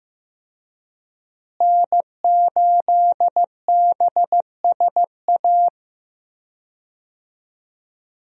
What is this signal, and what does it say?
Morse code, heard as a keyed tone, about 15 wpm, 700 Hz. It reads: N8BSA
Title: USA Call Signs - 5 Characters - Repeated 3x - 15wpm